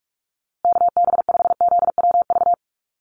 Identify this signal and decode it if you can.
Morse code, heard as a keyed tone, about 44 wpm, 700 Hz. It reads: KB5ZWV
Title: USA Call Signs Repeated 3x 45wpm